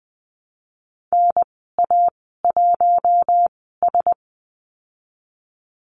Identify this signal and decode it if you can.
Morse code, heard as a keyed tone, about 19 wpm, spaced as for 14 wpm, 700 Hz. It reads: NA1S
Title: 076 Lesson 13g - Letter L - Call Signs 20wpm